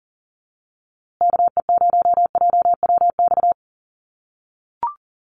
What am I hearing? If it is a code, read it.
Morse code, heard as a keyed tone, about 40 wpm, 700 Hz. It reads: KE0JWX
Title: USA Call Signs Repeated 2x 40wpm